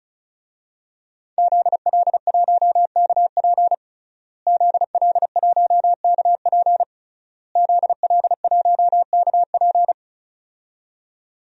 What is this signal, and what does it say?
Morse code, heard as a keyed tone, about 36 wpm, 700 Hz. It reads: ZL1KP ZL1KP ZL1KP
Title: DXCC-100 Call Signs - 5 Characters - Repeated 3x - 35wpm